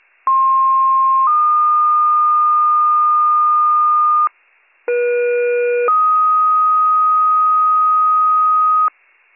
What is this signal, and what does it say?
Two-tone page, heard as a one-second tone, then a three-second tone, about 1060 then 1220 Hz, about 480 then 1250 Hz. It is heard from 0.3 to 8.9 s.